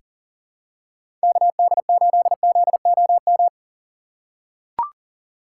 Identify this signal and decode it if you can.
Morse code, heard as a keyed tone, about 40 wpm, 700 Hz. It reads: KD8ZOM
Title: USA Call Signs - 6 Characters - 3x Character Spacing - 40wpm